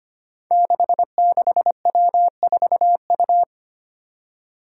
Morse code, heard as a keyed tone, about 25 wpm, 700 Hz. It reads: 66W4U
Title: Five Challenging Letter-Number Combinations 25wpm